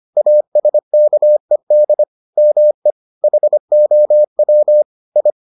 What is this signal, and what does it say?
Morse code, heard as a keyed tone, about 25 wpm, 600 Hz. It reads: ASKED ME HOW I